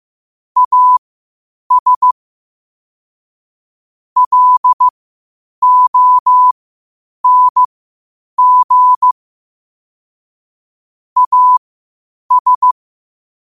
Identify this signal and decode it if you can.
Morse code, heard as a keyed tone, about 15 wpm, spaced as for 8 wpm, 1000 Hz. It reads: AS LONG AS
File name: qotdfiles / 8wpm